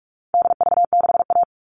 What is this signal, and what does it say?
Morse code, heard as a keyed tone, about 44 wpm, 700 Hz. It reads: DV6A
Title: All Cut Numbers - 4-Digit 45wpm